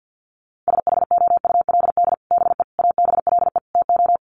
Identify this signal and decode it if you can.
Morse code, heard as a keyed tone, about 50 wpm, 700 Hz. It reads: SHOULD BE ABLE TO
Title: Sets of 4 Words 50wpm